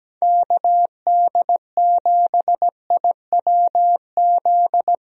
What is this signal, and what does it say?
Morse code, heard as a keyed tone, about 17 wpm, 700 Hz. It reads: KD7IWZ
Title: USA Call Signs - 6 Characters - 17wpm